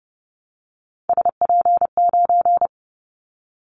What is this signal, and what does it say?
Morse code, heard as a keyed tone, about 30 wpm, 700 Hz. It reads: SP9
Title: Callsign Prefixes for Top 25 DXCC - 30wpm